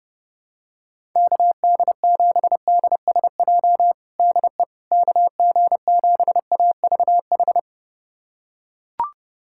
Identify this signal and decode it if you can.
Morse code, heard as a keyed tone, about 30 wpm, 700 Hz. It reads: KD7DSJ DE KG7AVH
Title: USA Callsign DE Callsign 30wpm